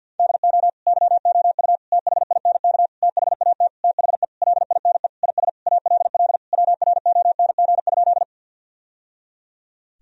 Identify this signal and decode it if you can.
Morse code, heard as a keyed tone, about 52 wpm, 700 Hz. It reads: DO YOU THINK THAT THE LINE IS ALL WRONG?
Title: Sentences from Top 1000 Words - Review Part 1 - 50wpm